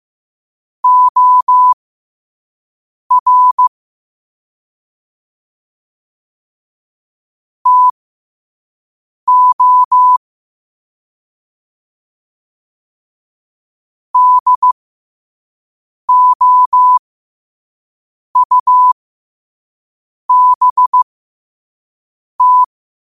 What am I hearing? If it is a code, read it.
Morse code, heard as a keyed tone, about 15 wpm, spaced as for 5 wpm, 1000 Hz. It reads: OR TO DOUBT
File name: qotdfiles / 5wpm